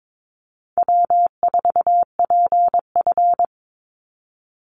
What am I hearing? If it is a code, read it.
Morse code, heard as a keyed tone, about 22 wpm, 700 Hz. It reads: W4PF
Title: Easy Call Signs - 22wpm